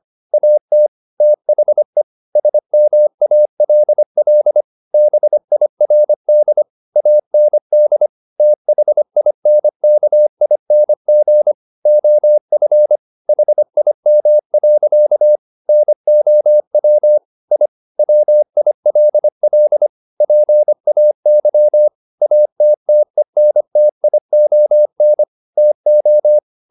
Morse code, heard as a keyed tone, about 25 wpm, 600 Hz. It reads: AT THE SMALL BIRD AND THINKING OF HIM. NOW I WILL PAY ATTENTION TO
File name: the old man and the sea part 1 25wpm